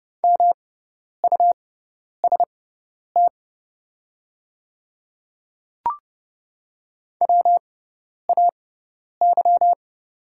Morse code, heard as a keyed tone, about 30 wpm, spaced as for 10 wpm, 700 Hz. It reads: MUST WAY
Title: 6x Character Spacing - Top 300 Words - 30wpm